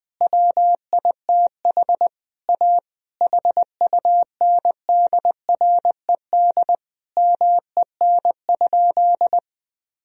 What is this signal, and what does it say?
Morse code, heard as a keyed tone, about 20 wpm, 700 Hz. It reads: WITH A HUNDRED MEN?
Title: Sentences from Top 1000 Words - Review Part 2 - 20wpm